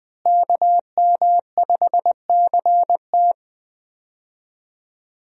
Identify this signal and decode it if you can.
Morse code, heard as a keyed tone, about 20 wpm, 700 Hz. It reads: KM5CT
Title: USA Call Signs - 5 Characters - Repeated 2x - 20wpm